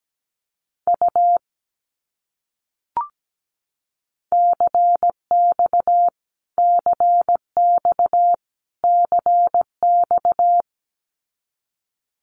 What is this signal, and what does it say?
Morse code, heard as a keyed tone, about 17 wpm, 700 Hz. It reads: U CX CX CX
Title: Abbreviations Repeated 3x - 17wpm